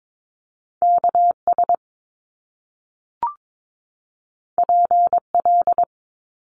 Morse code, heard as a keyed tone, about 22 wpm, 700 Hz. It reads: KS PL
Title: Top Two Letter Combinations in Words 22wpm